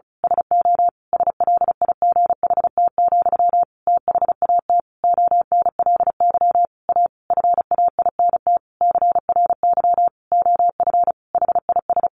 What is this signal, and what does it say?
Morse code, heard as a keyed tone, about 34 wpm, 700 Hz. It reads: SO SLIGHT, THAT ONLY A FAINT CRY OF HIS
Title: A Sound Of Thunder 35wpm